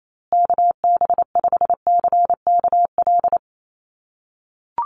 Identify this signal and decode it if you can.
Morse code, heard as a keyed tone, about 28 wpm, 700 Hz. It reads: KB5CKL
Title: USA Call Signs Repeated 3x 28wpm